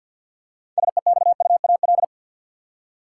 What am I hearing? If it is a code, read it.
Morse code, heard as a keyed tone, about 52 wpm, 700 Hz. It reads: SEXUAL
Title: Six Letter Words - 50wpm